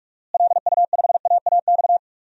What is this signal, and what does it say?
Morse code, heard as a keyed tone, about 46 wpm, 700 Hz. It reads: LU5AAX
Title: DXCC-100 Call Signs - 6 Characters - Repeated 3x - 45wpm